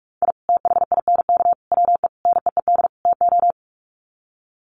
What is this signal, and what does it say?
Morse code, heard as a keyed tone, about 44 wpm, 700 Hz. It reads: I THINK WE NEED TO
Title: Sets of 5 Words 45wpm